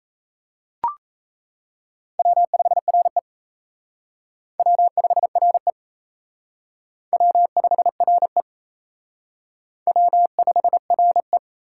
Morse code, heard as a keyed tone, about 36 wpm, 700 Hz. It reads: W5RE W5RE W5RE W5RE